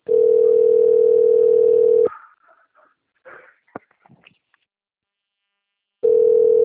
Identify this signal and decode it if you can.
Ringback tone (North America), 440 + 480 Hz.